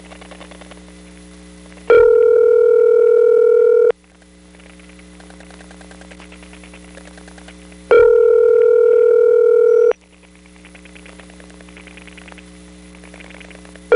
Ringback tone (North America), 440 + 480 Hz.
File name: GroupBell / MariettaMap-unfinished / Marietta1